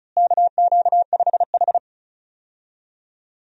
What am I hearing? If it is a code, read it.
Morse code, heard as a keyed tone, about 36 wpm, 700 Hz. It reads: KQ5H
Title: USA Call Signs - 4 Characters - Repeated 3x - 35wpm